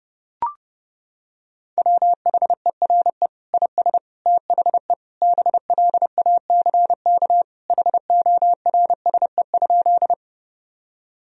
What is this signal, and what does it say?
Morse code, heard as a keyed tone, about 30 wpm, 700 Hz. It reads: WHERE IS THE BLACK HORSE?
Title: Sentences from Top 1000 Words - Review Part 2 - 30wpm